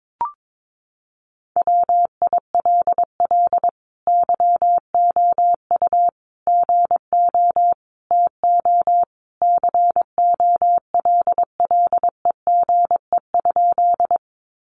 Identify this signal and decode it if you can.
Morse code, heard as a keyed tone, about 22 wpm, 700 Hz. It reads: WILL YOU GO TO COLLEGE?
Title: Sentences from Top 1300 Words 22wpm